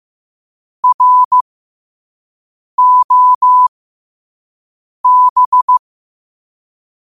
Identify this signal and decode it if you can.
Morse code, heard as a keyed tone, about 15 wpm, spaced as for 5 wpm, 1000 Hz. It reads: ROB
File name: qotdfiles / 5wpm